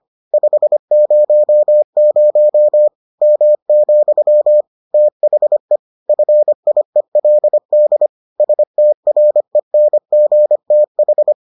Morse code, heard as a keyed tone, about 25 wpm, 600 Hz. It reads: 500 M, THE FIELD STRENGTH